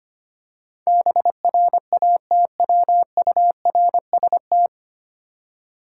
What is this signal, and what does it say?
Morse code, heard as a keyed tone, about 25 wpm, 700 Hz. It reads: BRATWURST
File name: practice / 25wpm / Difficult Words with Hints - 25wpm